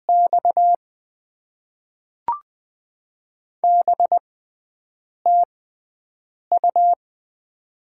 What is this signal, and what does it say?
Morse code, heard as a keyed tone, about 20 wpm, spaced as for 7 wpm, 700 Hz. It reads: X B TU
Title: Top 20 QSO Elements - 6x Character Spacing - 20wpm